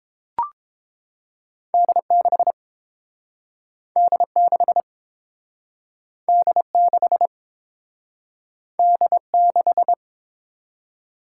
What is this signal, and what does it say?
Morse code, heard as a keyed tone, about 30 wpm, 700 Hz. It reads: D6 D6 D6 D6